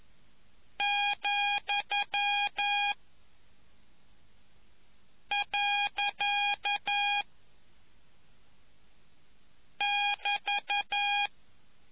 Morse code, heard as a keyed tone, about 11 wpm, 820 Hz. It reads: , . <BT>